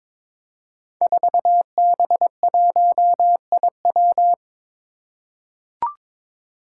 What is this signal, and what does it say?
Morse code, heard as a keyed tone, about 23 wpm, 700 Hz. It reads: 4B1IW
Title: Five Challenging Letter-Number Combinations 22wpm